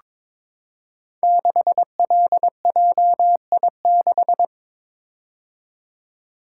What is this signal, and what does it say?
Morse code, heard as a keyed tone, about 23 wpm, 700 Hz. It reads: 6LJI6
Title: Code Groups - 5 Letter:Numbers - 22wpm